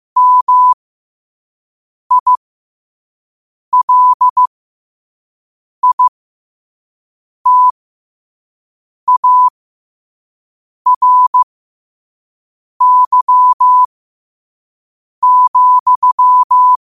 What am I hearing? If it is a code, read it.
Morse code, heard as a keyed tone, about 15 wpm, spaced as for 5 wpm, 1000 Hz. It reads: MILITARY,
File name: qotdfiles / 5wpm